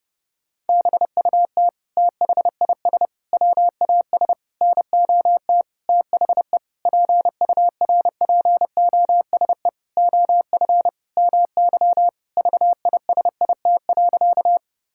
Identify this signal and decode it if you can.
Morse code, heard as a keyed tone, about 30 wpm, 700 Hz. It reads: BUT THIS WAS NOT THE PURPOSE OF MY VISIT.
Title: Sentences from Top 1000 Words - Review Part 6 - 30wpm